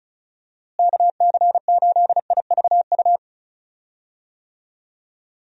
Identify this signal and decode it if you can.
Morse code, heard as a keyed tone, about 34 wpm, 700 Hz. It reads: KC8IVU
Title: USA Call Signs - 6 Characters - Repeated 3x - 35wpm